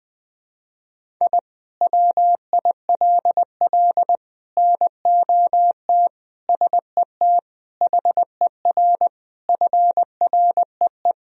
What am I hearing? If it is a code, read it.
Morse code, heard as a keyed tone, about 20 wpm, 700 Hz. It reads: I WILL NOT SET HER FREE
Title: Sentences from Top 1000 Words - Review Part 1 - 20wpm